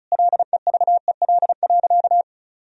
Morse code, heard as a keyed tone, about 34 wpm, 700 Hz. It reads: LEVEL.
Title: Sentences from Top 1000 Words - Review Part 4 - 35wpm